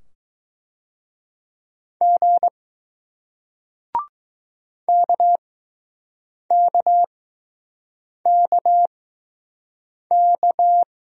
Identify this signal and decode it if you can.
Morse code, heard as a keyed tone, about 20 wpm, spaced as for 6 wpm, 700 Hz. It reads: G KKKK